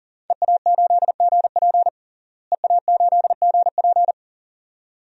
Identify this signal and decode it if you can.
Morse code, heard as a keyed tone, about 40 wpm, 700 Hz. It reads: EA8GP EA8GP
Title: DXCC-100 Call Signs - 5 Characters - Repeated 2x - 40wpm